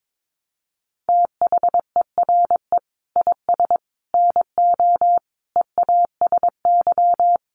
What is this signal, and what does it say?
Morse code, heard as a keyed tone, about 22 wpm, 700 Hz. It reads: THERE IS NO EASY
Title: Sentences from Top 1300 Words 22wpm